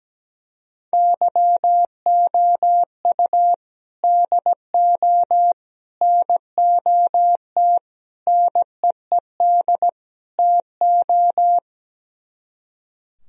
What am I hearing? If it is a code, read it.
Morse code, heard as a keyed tone, about 17 wpm, 700 Hz. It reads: YOU DO NOT NEED TO
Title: Speed-Racing - Sets of 5 Words 17wpm